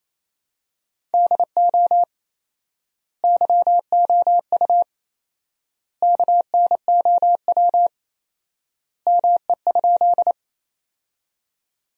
Morse code, heard as a keyed tone, about 28 wpm, 700 Hz. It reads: DO YOU KNOW ME?
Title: 4x Word-Spacing - Sentences from Top 200 Words - 28wpm